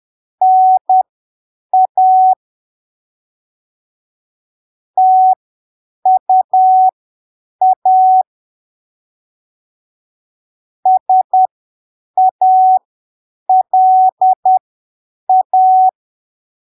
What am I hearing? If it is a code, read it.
Morse code, heard as a keyed tone, about 10 wpm, spaced as for 7 wpm, 750 Hz. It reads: NA TUA SALA
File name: arquivos / texto / texto 26 10ppm